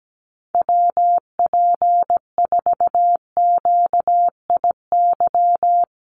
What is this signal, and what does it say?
Morse code, heard as a keyed tone, about 17 wpm, 700 Hz. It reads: WP4QIY
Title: USA Call Signs 17wpm